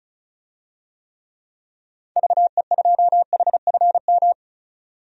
Morse code, heard as a keyed tone, about 36 wpm, 700 Hz. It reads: VE2HFM
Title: DXCC-100 Call Signs - 6 Characters - 35wpm